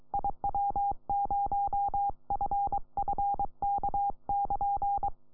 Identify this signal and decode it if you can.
Morse code, heard as a keyed tone, about 22 wpm, 800 Hz. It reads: IW0FFK<KN>
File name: iw0ffk / 2007 05 09 PJ4 PA3CNX IW0FFK